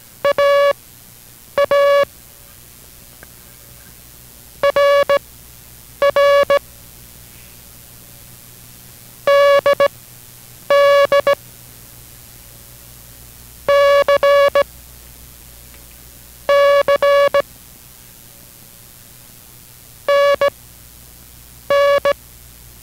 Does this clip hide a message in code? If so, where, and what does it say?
Yes: Morse code from 0.2 to 22.1 s, heard as a keyed tone, about 18 wpm, spaced as for 7 wpm, 570 Hz. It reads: AA RR DD CC NN